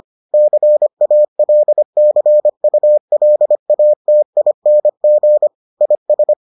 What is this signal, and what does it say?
Morse code, heard as a keyed tone, about 25 wpm, 600 Hz. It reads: CALCULATING IS